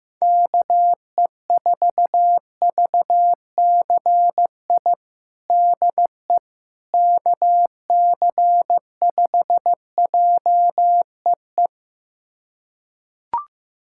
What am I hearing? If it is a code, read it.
Morse code, heard as a keyed tone, about 15 wpm, 700 Hz. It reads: KE4VCI DE KC5JEE
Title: USA Callsign DE Callsign - 6 Characters - 15wpm